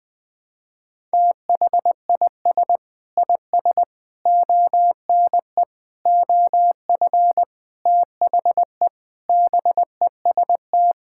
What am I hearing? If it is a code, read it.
Morse code, heard as a keyed tone, about 20 wpm, 700 Hz. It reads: THIS IS ONE OF THE BEST
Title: Sentences with 5-Word Phrases - 20wpm